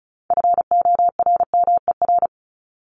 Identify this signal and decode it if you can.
Morse code, heard as a keyed tone, about 34 wpm, 700 Hz. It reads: FORMER
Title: Top 1751-2000 Words 35wpm